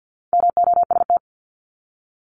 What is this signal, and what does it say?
Morse code, heard as a keyed tone, about 48 wpm, 700 Hz. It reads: MOST
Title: Top 200 Words 50wpm